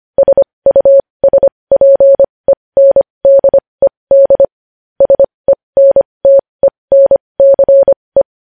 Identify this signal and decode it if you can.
Morse code, heard as a keyed tone, about 25 wpm, spaced as for 20 wpm, 550 Hz. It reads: SUSPENDED SENTENCE